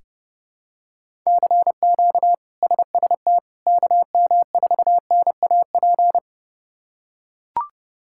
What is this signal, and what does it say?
Morse code, heard as a keyed tone, about 30 wpm, 700 Hz. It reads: CQ SST KM4NAP
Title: K1USN SST QSO Practice - 30wpm